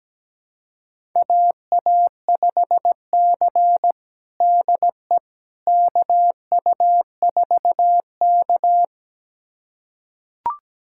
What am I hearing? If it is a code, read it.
Morse code, heard as a keyed tone, about 17 wpm, 700 Hz. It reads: AA5C DE KU4K
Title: USA Callsign DE Callsign - 4 Characters - 17wpm